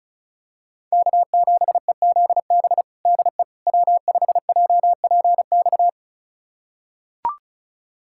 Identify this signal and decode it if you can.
Morse code, heard as a keyed tone, about 34 wpm, 700 Hz. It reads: K7EZB DE W5JPX
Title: USA Callsign DE Callsign - 5 Characters - 35wpm